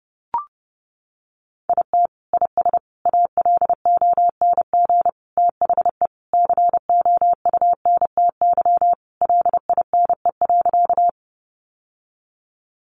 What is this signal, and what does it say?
Morse code, heard as a keyed tone, about 29 wpm, 700 Hz. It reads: IT IS ALONG THE COUNTY LINE.
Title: Sentences from Top 1200 Words 30wpm